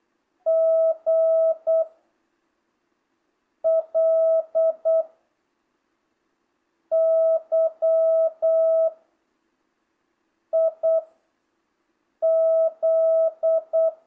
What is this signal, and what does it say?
Morse code, heard as a keyed tone, about 8 wpm, 660 Hz. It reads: G L Y I Z